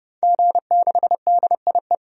Morse code, heard as a keyed tone, about 32 wpm, 700 Hz. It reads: G6DIE
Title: DXCC-100 Call Signs - 5 Characters - 30wpm